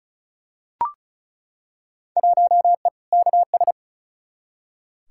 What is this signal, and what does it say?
Morse code, heard as a keyed tone, about 34 wpm, 700 Hz. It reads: 1E KS
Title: Field Day Exchanges - 35wpm